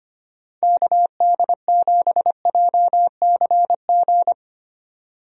Morse code, heard as a keyed tone, about 25 wpm, 700 Hz. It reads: KD7JCG
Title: USA Call Signs - 6 Characters - 25wpm